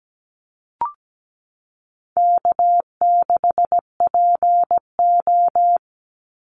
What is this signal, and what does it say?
Morse code, heard as a keyed tone, about 17 wpm, 700 Hz. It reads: K6PO
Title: Easy Call Signs - 17wpm